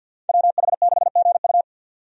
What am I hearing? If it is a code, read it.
Morse code, heard as a keyed tone, about 48 wpm, 700 Hz. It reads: WH6GU
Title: USA Call Signs - 5 Characters - 50wpm